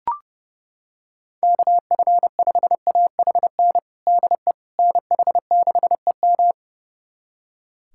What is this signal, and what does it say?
Morse code, heard as a keyed tone, about 30 wpm, 700 Hz. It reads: KF5AHN DE NH6EM
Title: USA Callsign DE Callsign 30wpm